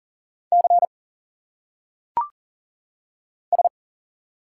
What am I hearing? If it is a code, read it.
Morse code, heard as a keyed tone, about 40 wpm, 700 Hz. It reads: C S
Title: Single Letters - Phonetic Pronunciation 40wpm